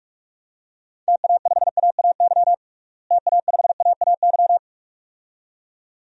Morse code, heard as a keyed tone, about 46 wpm, 700 Hz. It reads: TA5AAY TA5AAY
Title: DXCC-100 Call Signs - 6 Characters - Repeated 2x - 45wpm